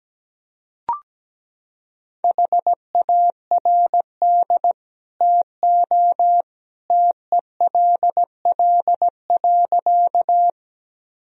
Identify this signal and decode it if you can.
Morse code, heard as a keyed tone, about 17 wpm, 700 Hz. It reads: HARD TO TELL.
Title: Sentences from Top 1000 Words - Review Part 1 - 17wpm